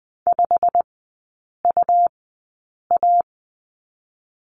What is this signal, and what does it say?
Morse code, heard as a keyed tone, about 20 wpm, 700 Hz. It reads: 5 U A
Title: 100 Lesson 17e - Letter u - Three Characters 20wpm